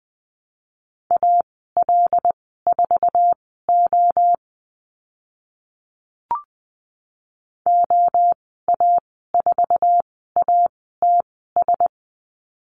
Morse code, heard as a keyed tone, about 20 wpm, spaced as for 15 wpm, 700 Hz. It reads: AL4O OA4ATS
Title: 076 Lesson 13g - Letter L - Call Signs 20wpm